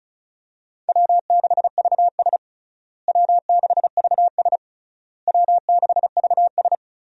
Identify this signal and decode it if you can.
Morse code, heard as a keyed tone, about 36 wpm, 700 Hz. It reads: W6VS W6VS W6VS
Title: USA Call Signs - 4 Characters - Repeated 3x - 35wpm